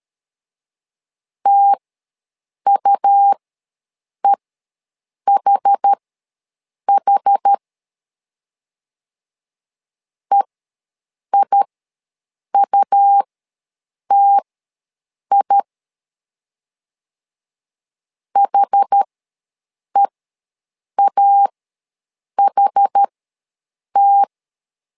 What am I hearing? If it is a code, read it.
Morse code, heard as a keyed tone, about 13 wpm, spaced as for 7 wpm, 800 Hz. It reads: TUEHH EIUTI HEAHT
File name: LEZIONI / 2 UHDN / 2 UH ws 12wpm 1730